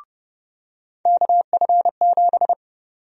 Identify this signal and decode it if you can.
Morse code, heard as a keyed tone, about 31 wpm, 700 Hz. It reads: KF7
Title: USA Call Sign Prefixes - 30wpm